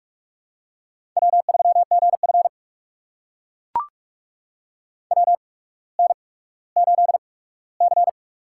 Morse code, heard as a keyed tone, about 42 wpm, spaced as for 19 wpm, 700 Hz. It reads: W3GF WN8C
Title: USA Call Signs - 4 Characters - 8x Character Spacing - 45wpm